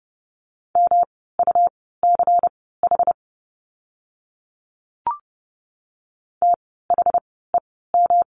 Morse code, heard as a keyed tone, about 30 wpm, spaced as for 17 wpm, 700 Hz. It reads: MUCH THEM
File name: course / 30:10wpm / 108 Lesson 18f - Letter m - Words 30wpm